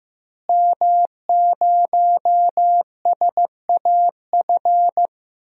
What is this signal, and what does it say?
Morse code, heard as a keyed tone, about 15 wpm, 700 Hz. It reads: M0SAF
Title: DXCC-100 Call Signs - 5 Characters - Repeated 2x - 15wpm